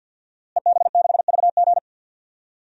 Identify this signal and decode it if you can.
Morse code, heard as a keyed tone, about 52 wpm, 700 Hz. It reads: EB6VG
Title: DXCC-100 Call Signs - 5 Characters - Repeated 2x - 50wpm